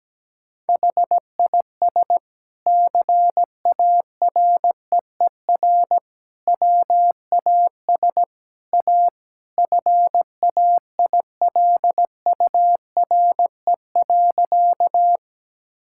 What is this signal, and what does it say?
Morse code, heard as a keyed tone, about 17 wpm, 700 Hz. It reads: HIS CAREER WAS A FAILURE.
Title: Sentences from Top 1000 Words - Review Part 6 - 17wpm